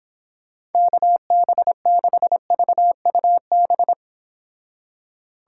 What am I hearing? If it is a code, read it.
Morse code, heard as a keyed tone, about 30 wpm, 700 Hz. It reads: KB6VUB